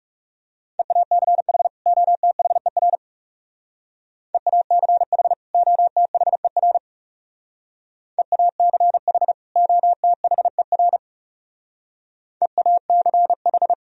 Morse code, heard as a keyed tone, about 38 wpm, 700 Hz. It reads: EACH OTHER EACH OTHER EACH OTHER EACH